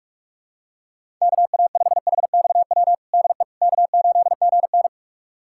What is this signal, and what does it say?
Morse code, heard as a keyed tone, about 43 wpm, 700 Hz. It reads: KA5HXW DE K8GN
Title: USA Callsign DE Callsign 45wpm